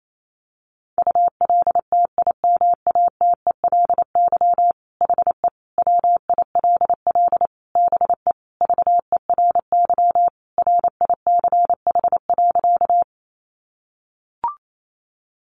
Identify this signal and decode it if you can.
Morse code, heard as a keyed tone, about 28 wpm, 700 Hz. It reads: ULTIMATELY HE WILL BE VERY RICH.
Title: Sentences from Top 1200 Words 28wpm